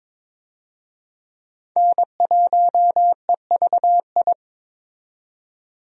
Morse code, heard as a keyed tone, about 22 wpm, 700 Hz. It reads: N1EVI
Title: USA Call Signs - 5 Characters - 22wpm